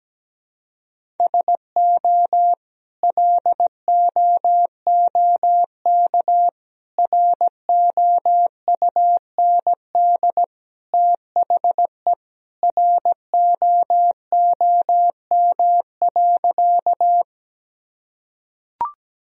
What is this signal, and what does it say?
Morse code, heard as a keyed tone, about 17 wpm, 700 Hz. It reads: SO LOOK ROUND THE ROOM.
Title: Sentences from Top 1000 Words - Review Part 1 - 17wpm